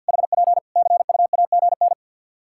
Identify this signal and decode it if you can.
Morse code, heard as a keyed tone, about 48 wpm, 700 Hz. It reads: HP CUAGN
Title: K1USN SST QSO Practice 50wpm